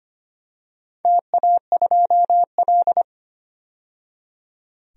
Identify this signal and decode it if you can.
Morse code, heard as a keyed tone, about 25 wpm, 700 Hz. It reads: TA2L
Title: DXCC-100 Call Signs - 4 Characters - 25wpm